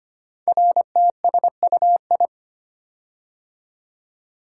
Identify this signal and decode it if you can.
Morse code, heard as a keyed tone, about 25 wpm, 700 Hz. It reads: RTSUI
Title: Code Groups - 5 Letters - 25wpm